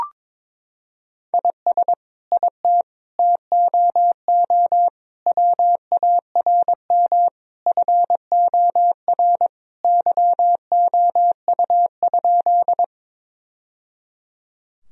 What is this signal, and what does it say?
Morse code, heard as a keyed tone, about 21 wpm, 700 Hz. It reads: IS IT TOO WARM FOR YOU?
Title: Sentences from Top 1000 Words - Review Part 2 - 22wpm